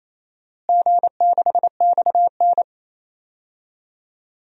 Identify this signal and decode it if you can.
Morse code, heard as a keyed tone, about 28 wpm, 700 Hz. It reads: G6XN
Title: DXCC-100 Call Signs - 4 Characters - Repeated 2x - 28wpm